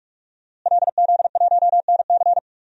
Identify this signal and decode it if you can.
Morse code, heard as a keyed tone, about 42 wpm, 700 Hz. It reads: LZ1NC
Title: DXCC-100 Call Signs - 5 Characters - Repeated 2x - 45wpm